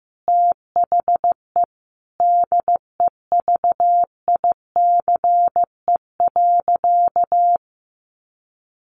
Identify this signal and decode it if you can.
Morse code, heard as a keyed tone, about 15 wpm, 700 Hz. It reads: THE DEVICE.
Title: Sentences from Top 1500 Words 15wpm